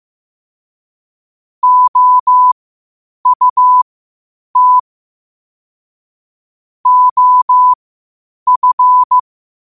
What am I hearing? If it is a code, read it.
Morse code, heard as a keyed tone, about 15 wpm, spaced as for 8 wpm, 1000 Hz. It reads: OUT OF
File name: qotdfiles / 8wpm